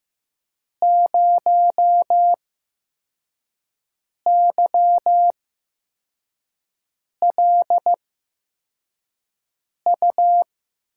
Morse code, heard as a keyed tone, about 15 wpm, spaced as for 4 wpm, 700 Hz. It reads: 0YLU
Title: USA Call Signs - 8x Character Spacing - 15wpm